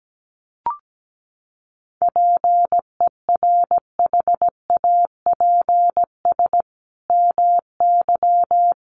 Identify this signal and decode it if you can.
Morse code, heard as a keyed tone, about 17 wpm, 700 Hz. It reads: PERHAPS MY
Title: Sentences from Top 1100 Words 17wpm